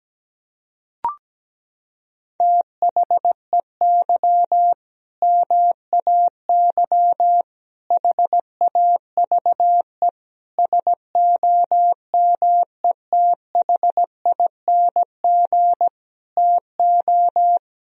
Morse code, heard as a keyed tone, about 17 wpm, 700 Hz. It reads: THEY MAY HAVE SOMETHING TO
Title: Sentences from Top 1000 Words - Review Part 2 - 17wpm